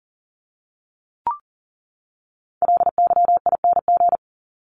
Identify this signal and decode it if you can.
Morse code, heard as a keyed tone, about 40 wpm, 700 Hz. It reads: LYING
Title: ING Suffix - Top 100 Words with Base Words 40wpm